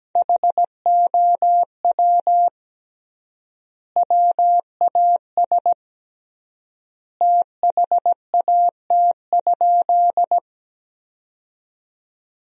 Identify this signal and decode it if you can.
Morse code, heard as a keyed tone, about 17 wpm, 700 Hz. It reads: HOW WAS THAT?
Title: 3x Word-Spacing - Sentences from Top 100 Words - 17wpm